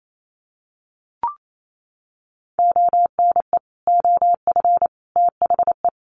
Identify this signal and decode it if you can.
Morse code, heard as a keyed tone, about 28 wpm, 700 Hz. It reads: ONE OF THE
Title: Sets of 4 Words 28wpm